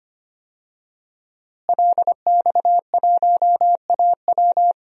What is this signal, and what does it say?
Morse code, heard as a keyed tone, about 25 wpm, 700 Hz. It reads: LX1AW
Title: DXCC-100 Call Signs - 5 Characters - Repeated 3x - 25wpm